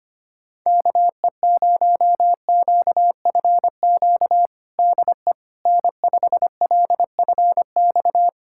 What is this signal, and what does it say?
Morse code, heard as a keyed tone, about 25 wpm, 700 Hz. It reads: KE0QFQ DE N5LFX
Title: USA Callsign DE Callsign 25wpm